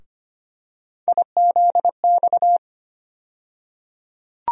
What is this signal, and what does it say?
Morse code, heard as a keyed tone, about 25 wpm, 700 Hz. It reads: IZX
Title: USA Call Sign Postfixes - 3 Letters 25wpm